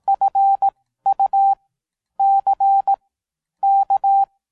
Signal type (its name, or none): Morse code